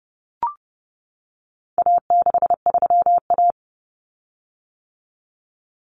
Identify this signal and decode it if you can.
Morse code, heard as a keyed tone, about 30 wpm, 700 Hz. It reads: A63A
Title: Common Cut Numbers - 4-Digit 30wpm